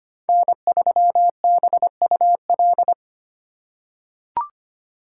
Morse code, heard as a keyed tone, about 25 wpm, 700 Hz. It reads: N3BUL
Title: USA Call Signs - 5 Characters - 3x Character Spacing - 25wpm